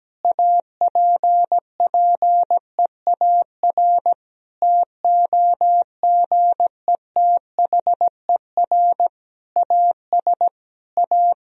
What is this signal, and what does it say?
Morse code, heard as a keyed tone, about 17 wpm, 700 Hz. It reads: APPEAR TOGETHER AS A
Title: Sentences from Top 1000 Words - Review Part 2 - 17wpm